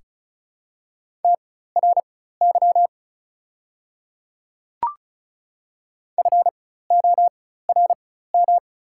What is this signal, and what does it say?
Morse code, heard as a keyed tone, about 33 wpm, spaced as for 16 wpm, 690 Hz. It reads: TRY FORM
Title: 4x Character Spacing - Top 300 Words - 35wpm